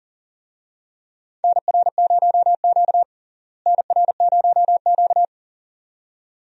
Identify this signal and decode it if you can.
Morse code, heard as a keyed tone, about 40 wpm, 700 Hz. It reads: NR0Q NR0Q
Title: USA Call Signs - 4 Characters - Repeated 2x - 40wpm